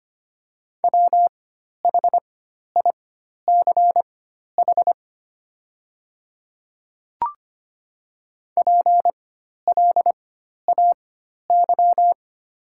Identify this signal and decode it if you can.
Morse code, heard as a keyed tone, about 25 wpm, spaced as for 12 wpm, 700 Hz. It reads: WHICH PLAY
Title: 4x Character Spacing - Top 500 Words - 25wpm